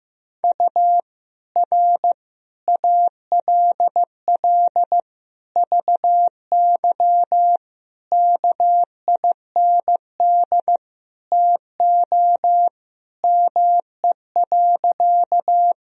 Morse code, heard as a keyed tone, about 15 wpm, 700 Hz. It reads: U R ALL VY KIND TO ME.
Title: Sentences from Top 200 Words and Abbreviations 15wpm